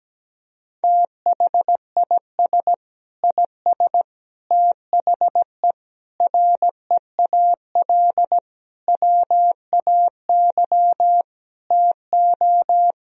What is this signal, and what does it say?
Morse code, heard as a keyed tone, about 17 wpm, 700 Hz. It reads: THIS IS THE REAL WAY TO
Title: Sentences from Top 1000 Words - Review Part 2 - 17wpm